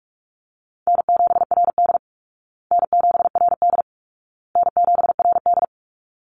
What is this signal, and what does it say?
Morse code, heard as a keyed tone, about 44 wpm, 700 Hz. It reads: N7RD N7RD N7RD
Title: Easy Call Signs Repeated 3x - 45wpm